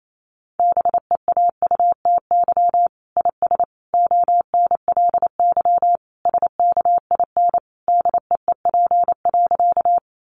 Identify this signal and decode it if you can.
Morse code, heard as a keyed tone, about 28 wpm, 700 Hz. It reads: BEAUTY IS ONLY SKIN DEEP.
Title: Sentences from Top 1300 Words 28wpm